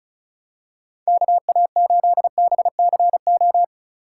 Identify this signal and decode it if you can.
Morse code, heard as a keyed tone, about 36 wpm, 700 Hz. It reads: KA8BCO
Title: USA Call Signs - 6 Characters - 2x Character Spacing - 35wpm